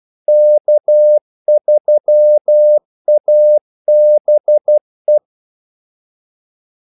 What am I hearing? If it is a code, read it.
Morse code, heard as a keyed tone, about 12 wpm, 600 Hz. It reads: K3ABE
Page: 2 ekstra ordmellemrum mellem kaldesignalerne